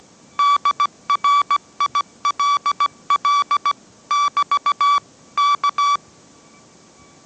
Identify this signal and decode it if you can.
Morse code, heard as a keyed tone, about 17 wpm, 1150 Hz. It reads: DRILL <BT> K
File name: mrx / mrx080 18wpm